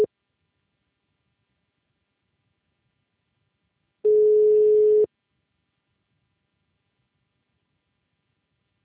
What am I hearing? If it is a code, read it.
Ringback tone (Europe, 425 Hz).